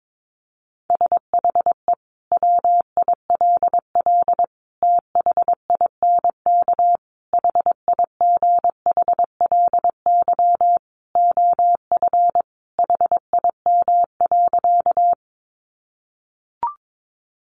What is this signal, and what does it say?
Morse code, heard as a keyed tone, about 22 wpm, 700 Hz. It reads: SHE WILL THINK HIGHLY OF HIM.
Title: Sentences from Top 1100 Words 22wpm